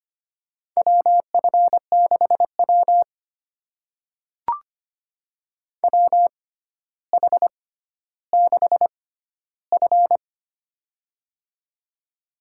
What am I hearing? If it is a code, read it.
Morse code, heard as a keyed tone, about 25 wpm, spaced as for 13 wpm, 700 Hz. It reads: WF6W WH6F
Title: USA Call Signs - 4 Characters - 6x Character Spacing - 25wpm